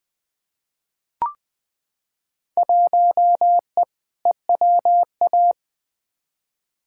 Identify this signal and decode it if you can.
Morse code, heard as a keyed tone, about 20 wpm, 700 Hz. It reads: 1E EWA
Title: Field Day Exchanges - 20wpm